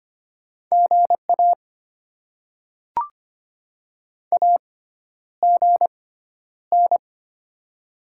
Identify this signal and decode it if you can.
Morse code, heard as a keyed tone, about 25 wpm, spaced as for 11 wpm, 700 Hz. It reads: GA AGN
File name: Top 20 QSO Elements - 6x Character Spacing - 25wpm